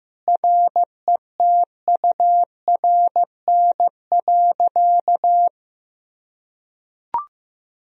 Morse code, heard as a keyed tone, about 16 wpm, 700 Hz. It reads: RETURN.
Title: Sentences from Top 1000 Words - Review Part 1 - 15wpm